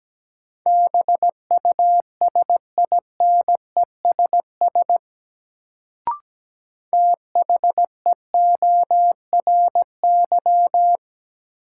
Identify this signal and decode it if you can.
Morse code, heard as a keyed tone, about 17 wpm, 700 Hz. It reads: BUSINESS THEORY
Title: Sentences from Top 600 Words - New Word Drill - 17wpm